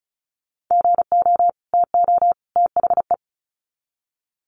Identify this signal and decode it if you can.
Morse code, heard as a keyed tone, about 34 wpm, 700 Hz. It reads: GO TO THE
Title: Sets of 3 Words 35wpm